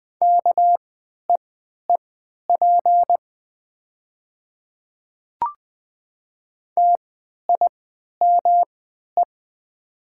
Morse code, heard as a keyed tone, about 20 wpm, spaced as for 11 wpm, 700 Hz. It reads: KEEP TIME
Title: 3x Character Spacing - Top 300 Words - 20wpm